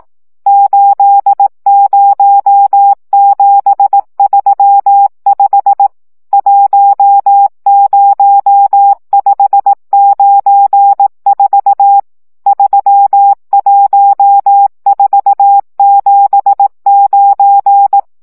Morse code, heard as a keyed tone, about 18 wpm, 800 Hz. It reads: 80735 10594 31479
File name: morze vizsga 2011-1 15